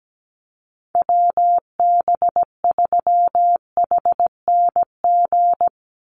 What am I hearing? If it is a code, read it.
Morse code, heard as a keyed tone, about 17 wpm, 700 Hz. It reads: WB3HNG
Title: USA Call Signs 17wpm